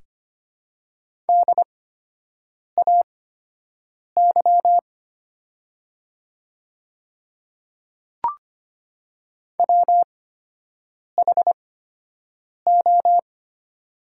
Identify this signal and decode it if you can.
Morse code, heard as a keyed tone, about 25 wpm, spaced as for 7 wpm, 700 Hz. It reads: DAY WHO
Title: 8x Character Spacing - Top 100 Words - 25wpm